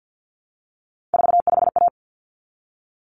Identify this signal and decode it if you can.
Morse code, heard as a keyed tone, about 50 wpm, 700 Hz. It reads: 45A
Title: Common Cut Numbers - 3-Digit 50wpm